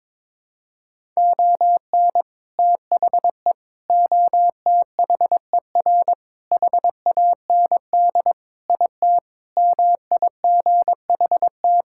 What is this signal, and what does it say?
Morse code, heard as a keyed tone, about 22 wpm, 700 Hz. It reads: ON THE OTHER HAND IT MIGHT
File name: practice / 22wpm / Sentences with 3-Word Phrases - 22wpm